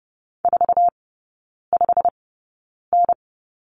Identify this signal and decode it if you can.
Morse code, heard as a keyed tone, about 30 wpm, spaced as for 9 wpm, 700 Hz. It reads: 45N
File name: Lesson 17e - Letter u - Three Characters 30wpm